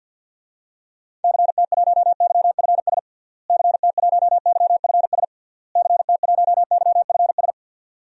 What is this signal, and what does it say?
Morse code, heard as a keyed tone, about 49 wpm, 700 Hz. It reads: CT1YFS CT1YFS CT1YFS
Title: DXCC-100 Call Signs - 6 Characters - Repeated 3x - 50wpm